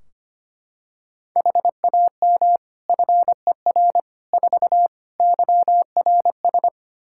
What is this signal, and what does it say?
Morse code, heard as a keyed tone, about 25 wpm, 700 Hz. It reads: HAM FER 4 YRS
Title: Practice QSO - Exchange 3 - More Info - 25wpm